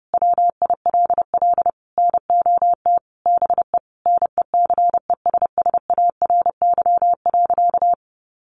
Morse code, heard as a keyed tone, about 30 wpm, 700 Hz. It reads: WILL NOT BE NECESSARY.
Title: Sentences from Top 800 Words 30wpm